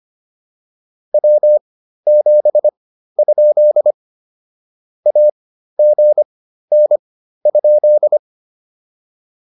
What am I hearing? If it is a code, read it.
Morse code, heard as a keyed tone, about 25 wpm, spaced as for 13 wpm, 600 Hz. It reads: W7? AGN?